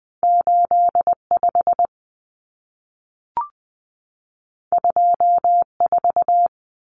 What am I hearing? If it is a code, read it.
Morse code, heard as a keyed tone, about 20 wpm, 700 Hz. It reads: 85 24
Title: Common Cut Numbers - 2-Digit 20wpm